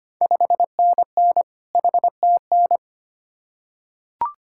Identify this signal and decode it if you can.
Morse code, heard as a keyed tone, about 25 wpm, 700 Hz. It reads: 5NN HTN
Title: Ontario QSO Party Exchange 25wpm